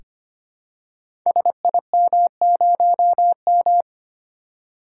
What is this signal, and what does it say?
Morse code, heard as a keyed tone, about 25 wpm, 700 Hz. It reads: SIM0M
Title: Code Groups - 5 Letter:Numbers - 25wpm